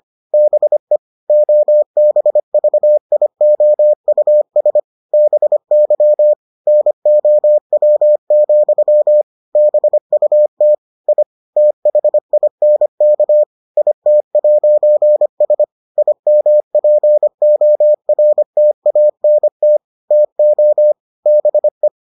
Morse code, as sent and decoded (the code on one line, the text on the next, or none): -... / --- -... ...- .. --- ..- ... / -... -.-- / -. --- .-- --..-- / -... ..- - / .. / - .... .. -. -.- / .. - .----. ... / .. -- .--. --- .-. - .- -. - / - --- / -... .
BE OBVIOUS BY NOW, BUT I THINK IT'S IMPORTANT TO BE